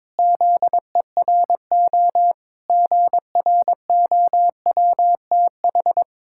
Morse code, heard as a keyed tone, about 22 wpm, 700 Hz. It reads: ZERO GROWTH
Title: Pangram Challenge v2 - 22wpm